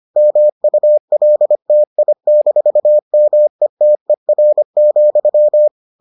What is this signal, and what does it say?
Morse code, heard as a keyed tone, about 25 wpm, 600 Hz. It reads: MULTI-METER,